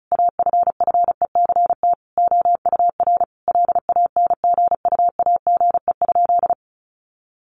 Morse code, heard as a keyed tone, about 34 wpm, 700 Hz. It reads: AFFECT OUR LANGUAGE?
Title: Sentences from Top 900 Words 35wpm